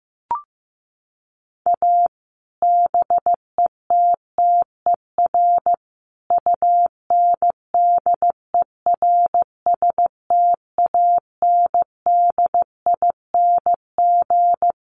Morse code, heard as a keyed tone, about 15 wpm, 700 Hz. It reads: A BETTER UNDERSTANDING
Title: Sets of 5 Words Encore2 15wpm